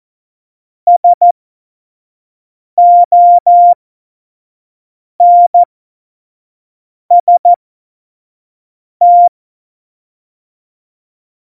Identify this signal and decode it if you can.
Morse code, heard as a keyed tone, about 14 wpm, spaced as for 5 wpm, 700 Hz. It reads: SONST
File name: Morsekurs / Witze / Witz 18. September 2018, 25 F70